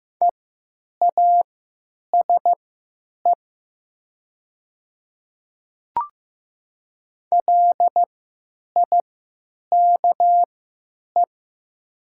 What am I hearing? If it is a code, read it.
Morse code, heard as a keyed tone, about 15 wpm, spaced as for 9 wpm, 700 Hz. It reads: EASE LIKE
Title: 3x Character Spacing - Top 500 Words - 15wpm